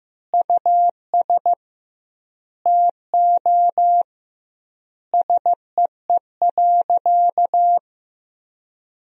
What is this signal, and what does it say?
Morse code, heard as a keyed tone, about 16 wpm, 700 Hz. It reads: US TO SEE.
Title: 2x Word-Spacing - Sentences from Top 500 Words - 15wpm